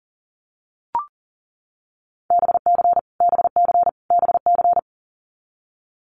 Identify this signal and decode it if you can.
Morse code, heard as a keyed tone, about 40 wpm, 700 Hz. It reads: BC BC BC
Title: Canadian Provinces and Territories - Repeated 3x - 40wpm